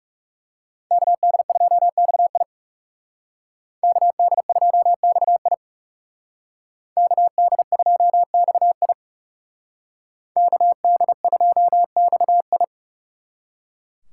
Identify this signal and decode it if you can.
Morse code, heard as a keyed tone, about 38 wpm, 700 Hz. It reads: KD2XI KD2XI KD2XI KD2XI